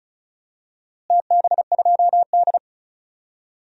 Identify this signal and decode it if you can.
Morse code, heard as a keyed tone, about 36 wpm, 700 Hz. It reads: TB2D
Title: DXCC-100 Call Signs - 4 Characters - Repeated 2x - 35wpm